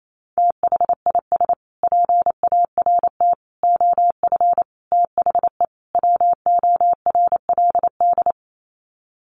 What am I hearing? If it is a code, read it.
Morse code, heard as a keyed tone, about 28 wpm, 700 Hz. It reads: THIS PART OF THE WORLD
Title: Sets of 5 Words Encore2 28wpm